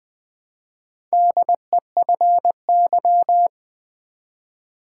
Morse code, heard as a keyed tone, about 20 wpm, 700 Hz. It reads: DEFY
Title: Build A Word - 1 Letter At A Time - 4-Letter Words - 20wpm